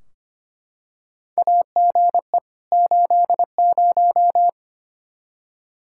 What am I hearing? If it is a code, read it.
Morse code, heard as a keyed tone, about 25 wpm, 700 Hz. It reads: AGE 80
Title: K1USN SST QSO Practice 25wpm